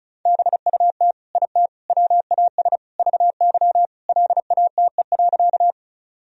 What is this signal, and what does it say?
Morse code, heard as a keyed tone, about 34 wpm, 700 Hz. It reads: BUT IT WAS VY LATE.
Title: Sentences from Top 300 Words and Abbreviations 35wpm